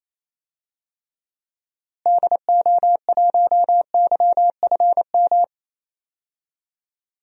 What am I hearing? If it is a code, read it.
Morse code, heard as a keyed tone, about 28 wpm, 700 Hz. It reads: DO1YFM
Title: DXCC-100 Call Signs - 6 Characters - Repeated 2x - 28wpm